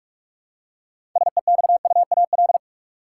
Morse code, heard as a keyed tone, about 43 wpm, 700 Hz. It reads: SEXUAL
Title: Six Letter Words - 45wpm